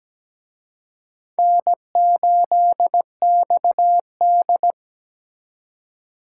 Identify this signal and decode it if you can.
Morse code, heard as a keyed tone, about 20 wpm, 700 Hz. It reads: N8XD
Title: Speed Racing - USA Call Signs 17wpm